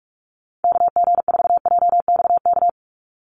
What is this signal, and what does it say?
Morse code, heard as a keyed tone, about 46 wpm, 690 Hz. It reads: KG4JXK
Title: USA Call Signs 45wpm